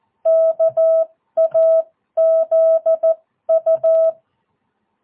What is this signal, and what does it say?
Morse code, heard as a keyed tone, about 14 wpm, spaced as for 12 wpm, 650 Hz. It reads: KAZU
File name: jh8cbh / 2022cw